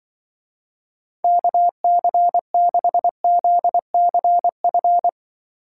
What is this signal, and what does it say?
Morse code, heard as a keyed tone, about 23 wpm, 700 Hz. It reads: KC6ZCF
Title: Speed Racing - USA 2x3 Call Signs 20wpm